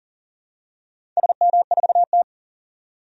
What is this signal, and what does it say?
Morse code, heard as a keyed tone, about 38 wpm, 700 Hz. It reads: SM4T
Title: DXCC-100 Call Signs - 4 Characters - 40wpm